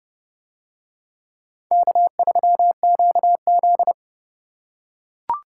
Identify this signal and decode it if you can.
Morse code, heard as a keyed tone, about 29 wpm, 700 Hz. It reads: K3QZ